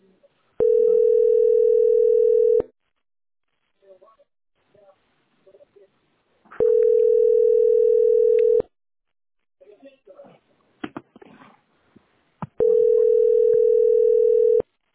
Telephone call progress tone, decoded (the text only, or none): ringback tone